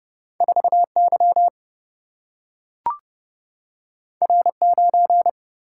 Morse code, heard as a keyed tone, about 30 wpm, 700 Hz. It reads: 4Y R9
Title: Single-Letter Single-Number 30wpm